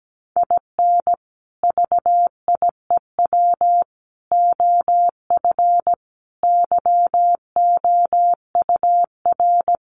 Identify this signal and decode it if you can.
Morse code, heard as a keyed tone, about 17 wpm, 700 Hz. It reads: IN VIEW OF YOUR
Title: Sentences from Top 900 Words 17wpm